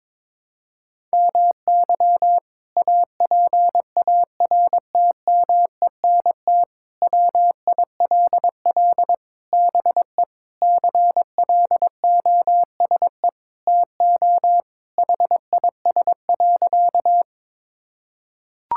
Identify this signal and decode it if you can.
Morse code, heard as a keyed tone, about 23 wpm, 700 Hz. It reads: MY APARTMENT WILL BE CLOSE TO HIS.
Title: Sentences from Top 1000 Words - Review Part 6 - 22wpm